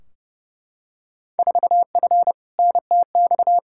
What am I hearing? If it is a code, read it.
Morse code, heard as a keyed tone, about 30 wpm, 700 Hz. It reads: 4F NTX
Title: Field Day Exchanges Encore - 30wpm